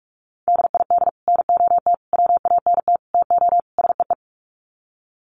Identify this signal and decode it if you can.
Morse code, heard as a keyed tone, about 44 wpm, 700 Hz. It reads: DID NOT WANT TO SEE
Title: Sets of 5 Words Encore2 45wpm